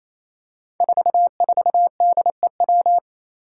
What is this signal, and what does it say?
Morse code, heard as a keyed tone, about 28 wpm, 700 Hz. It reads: 44DEW
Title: Five Challenging Letter-Number Combinations 28wpm